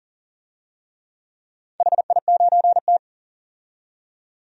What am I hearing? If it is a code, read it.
Morse code, heard as a keyed tone, about 40 wpm, 700 Hz. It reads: HI9T
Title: DXCC-100 Call Signs - 4 Characters - 40wpm